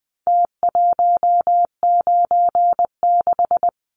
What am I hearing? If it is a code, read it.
Morse code, heard as a keyed tone, about 20 wpm, 700 Hz. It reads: T196
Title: Numbers - Contest Style - 4-Digit 20wpm